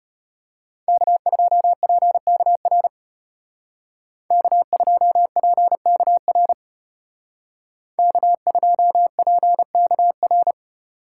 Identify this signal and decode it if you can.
Morse code, heard as a keyed tone, about 34 wpm, 700 Hz. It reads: K2PKR K2PKR K2PKR